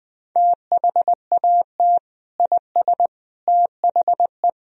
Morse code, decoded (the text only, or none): THAT IS THE